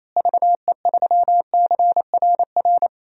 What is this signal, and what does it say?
Morse code, heard as a keyed tone, about 28 wpm, 700 Hz. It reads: VE3CRR
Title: DXCC-100 Call Signs - 6 Characters - Repeated 3x - 28wpm